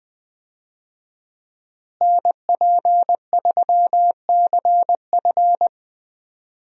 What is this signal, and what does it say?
Morse code, heard as a keyed tone, about 21 wpm, 700 Hz. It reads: NP3CF
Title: USA Call Signs - 5 Characters - Repeated 2x - 20wpm